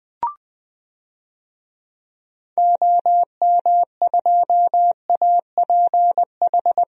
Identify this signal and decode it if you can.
Morse code, heard as a keyed tone, about 20 wpm, 700 Hz. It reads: OM2APH
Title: DXCC-100 Call Signs - 6 Characters - 20wpm